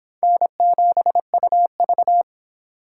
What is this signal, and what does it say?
Morse code, heard as a keyed tone, about 23 wpm, 700 Hz. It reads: N7UV